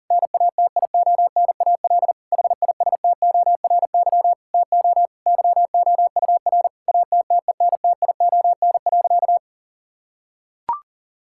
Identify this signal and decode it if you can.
Morse code, heard as a keyed tone, about 40 wpm, 700 Hz. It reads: NATIONAL HISTORY TO YOUR ATTENTION.
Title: Sentences from Top 1000 Words - Review Part 6 - 40wpm